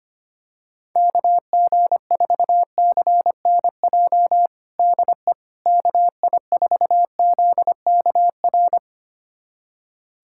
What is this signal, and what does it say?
Morse code, heard as a keyed tone, about 25 wpm, 700 Hz. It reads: KG4CNJ DE KI4ZKR
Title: USA Callsign DE Callsign - 6 Characters - 25wpm